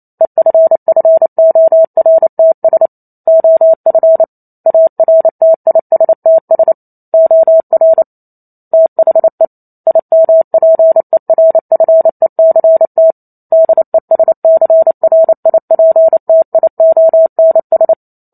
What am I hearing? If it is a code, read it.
Morse code, heard as a keyed tone, about 28 wpm, 650 Hz. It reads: EFFORTS OF ARTISTS OR THE IMPERFECT DESCRIPTIONS